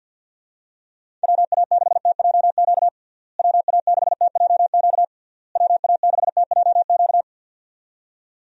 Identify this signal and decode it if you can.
Morse code, heard as a keyed tone, about 49 wpm, 700 Hz. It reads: WA6TJQ WA6TJQ WA6TJQ
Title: USA Call Signs - 6 Characters - Repeated 3x - 50wpm